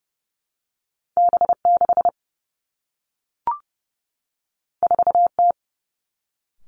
Morse code, heard as a keyed tone, about 30 wpm, 700 Hz. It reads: B 6 4 T
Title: Two Challenging Letter-Number Combinations 30wpm